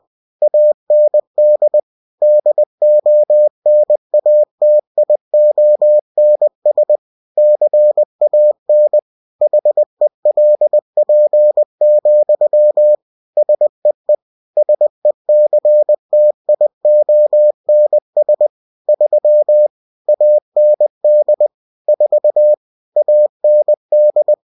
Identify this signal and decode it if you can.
Morse code, heard as a keyed tone, about 20 wpm, 600 Hz. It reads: AND DONATIONS CAN HELP, SEE SECTIONS 3 AND 4 AND
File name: cw-books / winnie-the-pooh / winnie-20wpm-page-0120